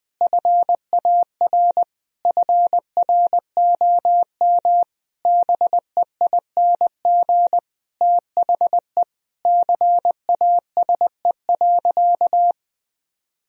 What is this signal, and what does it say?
Morse code, heard as a keyed tone, about 20 wpm, 700 Hz. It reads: FAR FROM BEING THE CASE.
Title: Sentences from Top 1000 Words - Review Part 3 - 20wpm